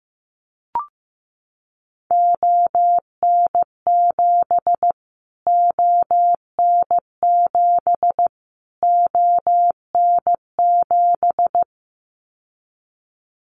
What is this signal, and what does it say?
Morse code, heard as a keyed tone, about 15 wpm, 700 Hz. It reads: ON7 ON7 ON7
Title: Callsign Prefixes for Top 25 DXCC - Repeated 3x - 15wpm